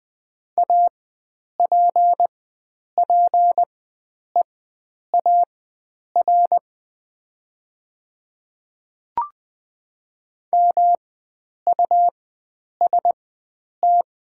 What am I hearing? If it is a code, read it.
Morse code, heard as a keyed tone, about 20 wpm, spaced as for 9 wpm, 700 Hz. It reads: APPEAR MUST
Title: 4x Character Spacing - Top 500 Words - 20wpm